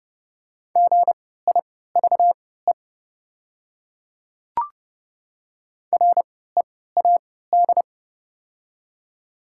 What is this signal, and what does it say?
Morse code, heard as a keyed tone, about 30 wpm, spaced as for 17 wpm, 690 Hz. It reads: GIVE READ
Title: 3x Character Spacing - Top 200 Words - 30wpm